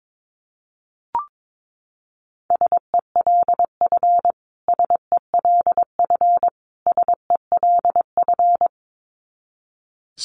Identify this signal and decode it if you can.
Morse code, heard as a keyed tone, about 22 wpm, 700 Hz. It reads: SELF SELF SELF
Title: Top 200 Words Repeated 22wpm